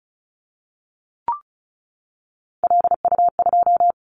Morse code, heard as a keyed tone, about 36 wpm, 690 Hz. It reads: LU2
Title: Callsign Prefixes for Top 50 DXCC - 35wpm